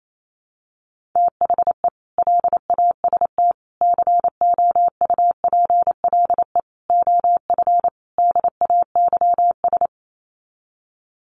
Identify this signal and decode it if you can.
Morse code, heard as a keyed tone, about 28 wpm, 700 Hz. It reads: THE LAST COUPLE OF DAYS
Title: Sets of 5 Words Encore 28wpm